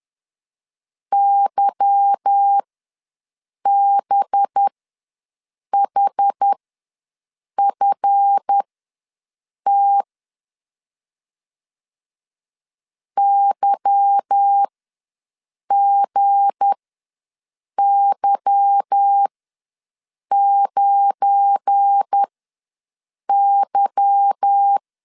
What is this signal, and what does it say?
Morse code, heard as a keyed tone, about 10 wpm, spaced as for 6 wpm, 800 Hz. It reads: YBHFT YGY9Y